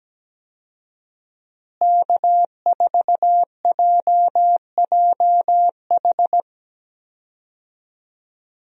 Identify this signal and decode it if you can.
Morse code, heard as a keyed tone, about 17 wpm, 700 Hz. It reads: K4JJH
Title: USA Call Signs - 5 Characters - Repeated 2x - 17wpm